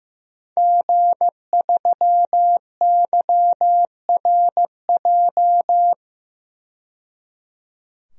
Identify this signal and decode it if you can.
Morse code, heard as a keyed tone, about 15 wpm, 700 Hz. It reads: G3YRJ
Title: DXCC-100 Call Signs - 5 Characters - 15wpm